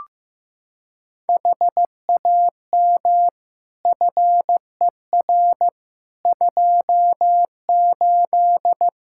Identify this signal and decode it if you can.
Morse code, heard as a keyed tone, about 15 wpm, 700 Hz. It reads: HAM FER 28
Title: Practice QSO - Exchange 3 - More Info - 15wpm